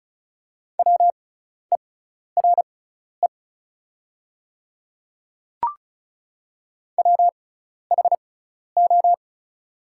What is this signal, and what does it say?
Morse code, heard as a keyed tone, about 34 wpm, spaced as for 12 wpm, 700 Hz. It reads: WERE WHO
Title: 6x Character Spacing - Top 100 Words - 35wpm